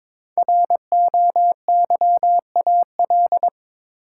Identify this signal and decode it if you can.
Morse code, heard as a keyed tone, about 22 wpm, 700 Hz. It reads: ROYAL
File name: 5 Letter Words - First 3 Letters Given - 22wpm